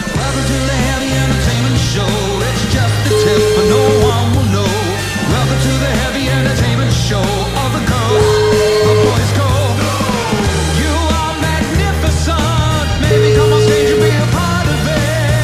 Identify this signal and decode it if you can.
Ringback tone (Europe, 425 Hz), with noise.